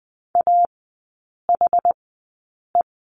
Morse code, heard as a keyed tone, about 20 wpm, spaced as for 8 wpm, 690 Hz. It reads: AHE
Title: 081 Lesson 14e - Number 2 - Three Characters 20wpm